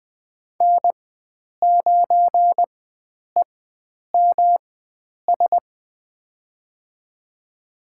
Morse code, heard as a keyed tone, about 21 wpm, spaced as for 9 wpm, 700 Hz. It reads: N9EMS
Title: USA Call Signs - 5 Characters - 4x Character Spacing - 20wpm